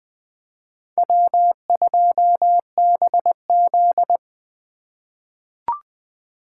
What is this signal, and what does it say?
Morse code, heard as a keyed tone, about 20 wpm, 700 Hz. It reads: W2BZ
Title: USA Call Signs - 4 Characters - 2x Character Spacing - 20wpm